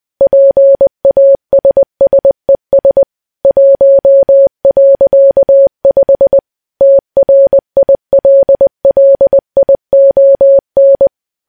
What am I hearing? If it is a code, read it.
Morse code, heard as a keyed tone, about 20 wpm, 550 Hz. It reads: PASSES 1.5 TRILLION
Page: news-headlines-in-morse-code-at-20-wpm